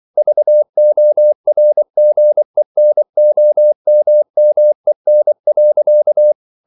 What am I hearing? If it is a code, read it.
Morse code, heard as a keyed tone, about 24 wpm, 600 Hz. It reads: VORGENOMMEN.